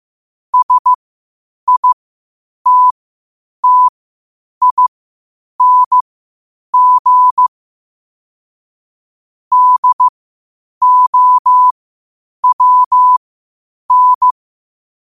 Morse code, heard as a keyed tone, about 15 wpm, spaced as for 8 wpm, 1000 Hz. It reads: SITTING DOWN
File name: qotdfiles / 8wpm